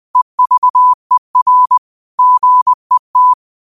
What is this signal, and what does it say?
Morse code, heard as a keyed tone, about 20 wpm, 1000 Hz. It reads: EVER GET
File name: qotdfiles / 20wpm